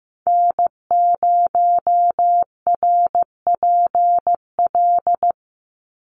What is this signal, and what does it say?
Morse code, heard as a keyed tone, about 15 wpm, 700 Hz. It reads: N0RPL
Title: USA Call Signs 15wpm